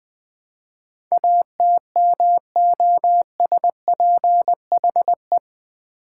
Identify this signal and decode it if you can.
Morse code, heard as a keyed tone, about 20 wpm, 700 Hz. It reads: ATMOSPHE
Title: Build A Word - 4 Letters At A Time - 10-Letter Words - 20wpm